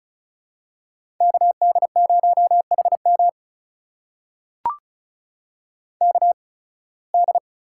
Morse code, heard as a keyed tone, about 34 wpm, spaced as for 23 wpm, 700 Hz. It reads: KD0HM KD
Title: USA Call Signs - 5 Characters - 8x Character Spacing - 35wpm